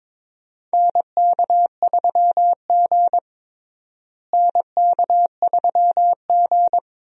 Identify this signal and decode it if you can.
Morse code, heard as a keyed tone, about 23 wpm, 700 Hz. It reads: NK3G NK3G
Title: USA Call Signs - 4 Characters - Repeated 2x - 22wpm